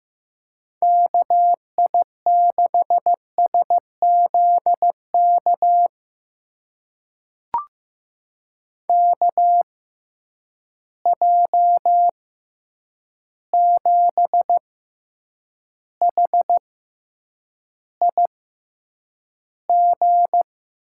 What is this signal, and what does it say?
Morse code, heard as a keyed tone, about 15 wpm, spaced as for 8 wpm, 700 Hz. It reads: KI6SZK KJ7HIG
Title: USA Call Signs - 6 Characters - 6x Character Spacing - 15wpm